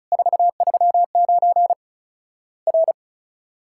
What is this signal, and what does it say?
Morse code, heard as a keyed tone, about 36 wpm, 700 Hz. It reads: 439 R
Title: Worked All Europe DX Contest - QTC Exchange - 35wpm